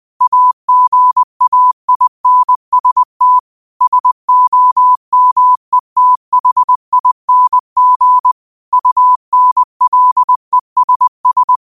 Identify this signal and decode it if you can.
Morse code, heard as a keyed tone, about 20 wpm, 1000 Hz. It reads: AGAINST SOMETHING UNLESS